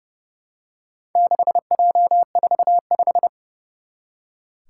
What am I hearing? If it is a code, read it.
Morse code, heard as a keyed tone, about 30 wpm, 700 Hz. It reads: 6J45
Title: Four Challenging Letter-Number Combinations 30wpm